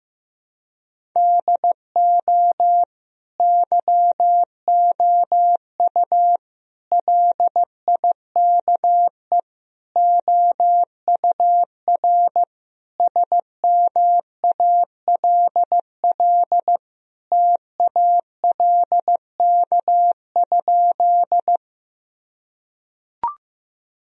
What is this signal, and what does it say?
Morse code, heard as a keyed tone, about 15 wpm, 700 Hz. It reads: DO YOU LIKE OUR SMALL TALK?
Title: Sentences from Top 1000 Words - Review Part 1 - 15wpm